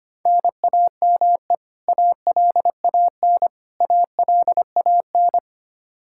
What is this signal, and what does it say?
Morse code, heard as a keyed tone, about 25 wpm, 700 Hz. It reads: NAME ALAN ALAN
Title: Practice QSO - Exchange 1 - Beginning - 25wpm